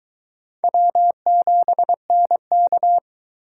Morse code, heard as a keyed tone, about 22 wpm, 700 Hz. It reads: W7NK